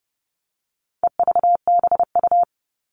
Morse code, heard as a keyed tone, about 30 wpm, 700 Hz. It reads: EVBU
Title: All Cut Numbers - 4-Digit 30wpm